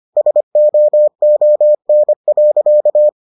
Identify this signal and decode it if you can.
Morse code, heard as a keyed tone, about 25 wpm, 600 Hz. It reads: SOON.